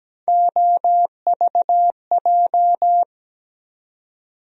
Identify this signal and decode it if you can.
Morse code, heard as a keyed tone, about 17 wpm, 700 Hz. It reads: OVJ
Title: USA Call Sign Postfixes - 3 Letters 17wpm